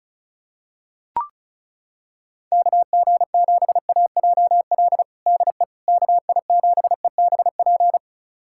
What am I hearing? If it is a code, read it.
Morse code, heard as a keyed tone, about 36 wpm, 700 Hz. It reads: KG7AJL DE KI7EBP
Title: USA Callsign DE Callsign - 6 Characters - 35wpm